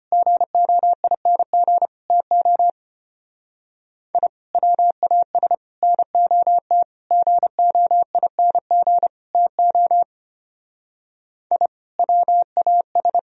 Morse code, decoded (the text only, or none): GOING TO I WAS NOT GOING TO I WAS